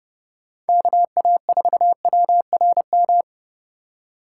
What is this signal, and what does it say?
Morse code, heard as a keyed tone, about 29 wpm, 700 Hz. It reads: KA4WRM
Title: Speed Racing - USA 2x3 Call Signs 25wpm